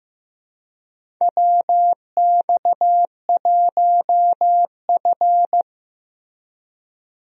Morse code, heard as a keyed tone, about 15 wpm, 700 Hz. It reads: WX1F
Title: USA Call Signs - 4 Characters - Repeated 3x - 15wpm